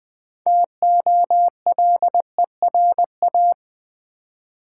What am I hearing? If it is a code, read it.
Morse code, heard as a keyed tone, about 20 wpm, 700 Hz. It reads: TOLERA
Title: Build A Word - 2 Letters At A Time - 8-Letter Words - 20wpm